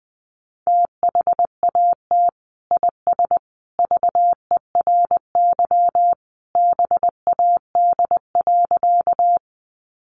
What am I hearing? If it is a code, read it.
Morse code, heard as a keyed tone, about 20 wpm, 700 Hz. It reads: THAT IS VERY BAD.
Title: Sentences from Top 700 Words 20wpm